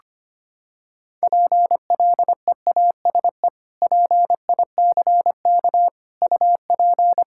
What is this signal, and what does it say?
Morse code, heard as a keyed tone, about 25 wpm, 700 Hz. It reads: PLEASE PICK UP
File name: practice / 25wpm / Sentences with 2-Word Phrases - 25wpm